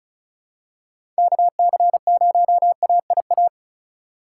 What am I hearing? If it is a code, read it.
Morse code, heard as a keyed tone, about 34 wpm, 700 Hz. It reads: KC0AIA
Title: USA Call Signs - 6 Characters - 4x Character Spacing - 35wpm